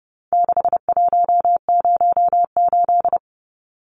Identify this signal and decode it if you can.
Morse code, heard as a keyed tone, about 30 wpm, 700 Hz. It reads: 6108
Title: Numbers - 4-Digit 30wpm